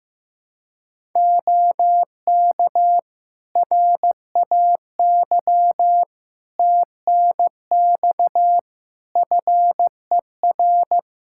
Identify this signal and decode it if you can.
Morse code, heard as a keyed tone, about 15 wpm, 700 Hz. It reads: OK RAY TNX FER
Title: Practice QSO - Exchange 4 - Ending - 15wpm